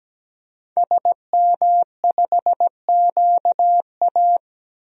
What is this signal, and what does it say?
Morse code, heard as a keyed tone, about 17 wpm, 700 Hz. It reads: SM5QA
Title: DXCC-100 Call Signs - 5 Characters - Repeated 2x - 17wpm